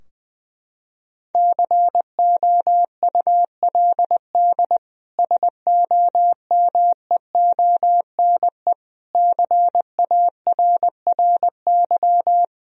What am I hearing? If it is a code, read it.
Morse code, heard as a keyed tone, about 21 wpm, 700 Hz. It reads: COULD SOMEONE CARRY
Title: Sentences from Top 1000 Words - Review Part 4 - 20wpm